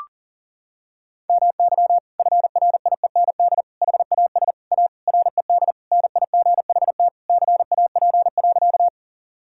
Morse code, heard as a keyed tone, about 40 wpm, 700 Hz. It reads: MY FRIEND HAS A RED NIGHT CAP.
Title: Sentences from Top 1600 Words 40wpm